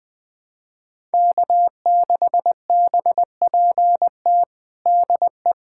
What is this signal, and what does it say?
Morse code, heard as a keyed tone, about 20 wpm, 700 Hz. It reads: K6BPT DE
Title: USA Callsign DE Callsign - 5 Characters - 20wpm